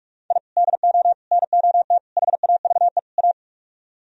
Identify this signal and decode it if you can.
Morse code, heard as a keyed tone, about 46 wpm, 700 Hz. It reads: I DO NOT HAVE A